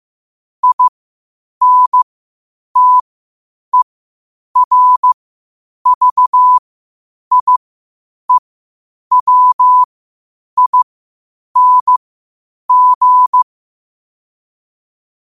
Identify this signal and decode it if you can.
Morse code, heard as a keyed tone, about 15 wpm, spaced as for 8 wpm, 1000 Hz. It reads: INTERVIEWING
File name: 8wpm